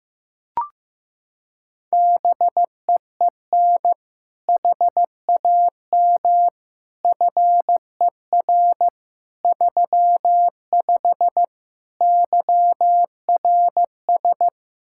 Morse code, heard as a keyed tone, about 16 wpm, 700 Hz. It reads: BEEN HAM FER 35 YRS
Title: Practice QSO - Exchange 3 - More Info - 15wpm